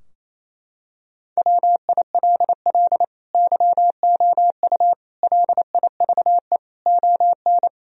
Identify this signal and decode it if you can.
Morse code, heard as a keyed tone, about 27 wpm, 700 Hz. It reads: WILL YOU LIVE ON
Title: Sentences from Top 1000 Words - Review Part 3 - 28wpm